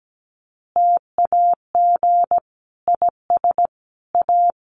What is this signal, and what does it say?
Morse code, heard as a keyed tone, about 17 wpm, 700 Hz. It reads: TAG IS A
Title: Additional Easy Phrases 17wpm